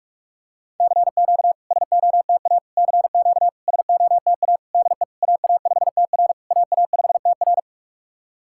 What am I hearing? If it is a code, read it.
Morse code, heard as a keyed tone, about 46 wpm, 700 Hz. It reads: CQ SOTA CQ SOTA DE AA5TR AA5TR
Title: SOTA QSO Practice 45wpm